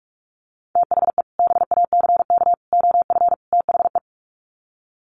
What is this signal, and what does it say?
Morse code, heard as a keyed tone, about 44 wpm, 700 Hz. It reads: THE BACK OF THE
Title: Sets of 4 Words 45wpm